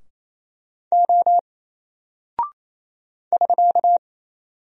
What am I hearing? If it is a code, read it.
Morse code, heard as a keyed tone, about 28 wpm, 700 Hz. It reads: O <SK>
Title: Single Character (Letter:Number:Punctuation:Prosign) Alt 28wpm